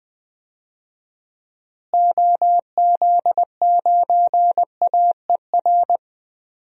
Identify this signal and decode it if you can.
Morse code, heard as a keyed tone, about 21 wpm, 700 Hz. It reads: OZ9AER
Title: DXCC-100 Call Signs - 6 Characters - Repeated 2x - 20wpm